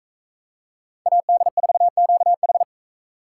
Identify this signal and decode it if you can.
Morse code, heard as a keyed tone, about 44 wpm, 700 Hz. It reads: AD4QH